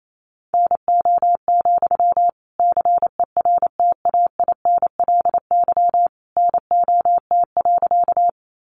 Morse code, heard as a keyed tone, about 28 wpm, 700 Hz. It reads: NO, CERTAINLY NOT.
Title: Sentences from Top 900 Words 28wpm